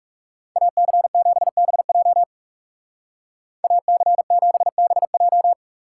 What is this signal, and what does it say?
Morse code, heard as a keyed tone, about 37 wpm, 700 Hz. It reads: AC7BJ AC7BJ
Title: Speed Racing - USA Call Signs 30wpm